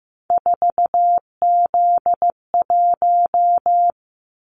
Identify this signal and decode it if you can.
Morse code, heard as a keyed tone, about 15 wpm, 700 Hz. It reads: 4Z1
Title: Callsign Prefixes for Top 50 DXCC - 15wpm